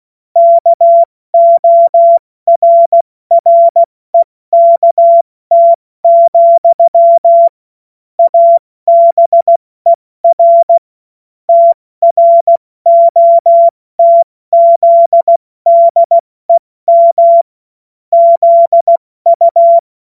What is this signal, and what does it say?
Morse code, heard as a keyed tone, about 16 wpm, 680 Hz. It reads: KORREKT, ABER TROTZDEM ZU
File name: Morsekurs / Witze / Witz 16. August 70 F80